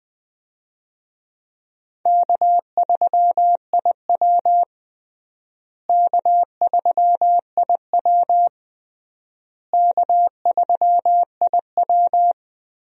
Morse code, heard as a keyed tone, about 21 wpm, 700 Hz. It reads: K3IW K3IW K3IW
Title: USA Call Signs - 4 Characters - Repeated 3x - 20wpm